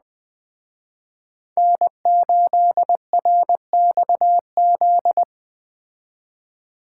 Morse code, heard as a keyed tone, about 21 wpm, 700 Hz. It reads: N8RXZ
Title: Speed Racing - USA 1x3 Call Signs 17wpm